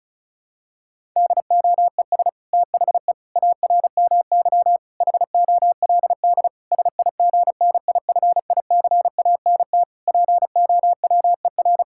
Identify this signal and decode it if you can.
Morse code, heard as a keyed tone, about 34 wpm, 700 Hz. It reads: DOES THE ARMY HOLD SIGNIFICANT POWER
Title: Sentences from Top 1000 Words - Review Part 6 - 35wpm